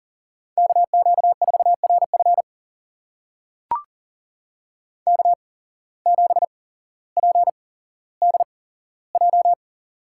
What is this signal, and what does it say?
Morse code, heard as a keyed tone, about 39 wpm, spaced as for 17 wpm, 700 Hz. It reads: KQ4RF K7PDJ
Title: USA Call Signs - 5 Characters - 8x Character Spacing - 40wpm